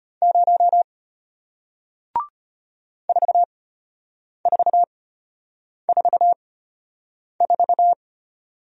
Morse code, heard as a keyed tone, about 33 wpm, spaced as for 8 wpm, 700 Hz. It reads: 0 4444